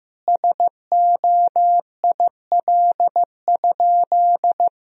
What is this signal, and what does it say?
Morse code, heard as a keyed tone, about 15 wpm, 700 Hz. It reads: SOIL?
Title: Sentences from Top 1000 Words - Review Part 4 - 15wpm